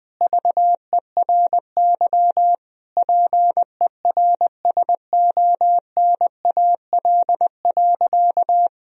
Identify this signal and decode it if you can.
Morse code, heard as a keyed tone, about 21 wpm, 700 Hz. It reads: VERY PERSONAL.
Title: Sentences from Top 1000 Words - Review Part 4 - 20wpm